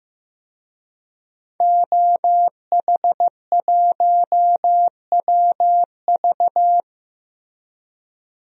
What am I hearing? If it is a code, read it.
Morse code, heard as a keyed tone, about 15 wpm, 700 Hz. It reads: OH1WV
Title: DXCC-100 Call Signs - 5 Characters - 15wpm